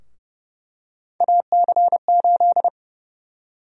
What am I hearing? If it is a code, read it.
Morse code, heard as a keyed tone, about 30 wpm, 700 Hz. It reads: AC8
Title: USA Call Sign Prefixes - 30wpm